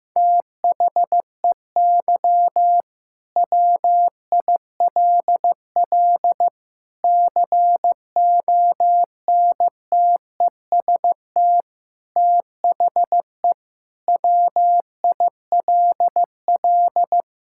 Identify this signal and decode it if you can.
Morse code, heard as a keyed tone, about 15 wpm, 700 Hz. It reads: THEY WILL CONTEST THE WILL
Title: Sentences from Top 1600 Words 15wpm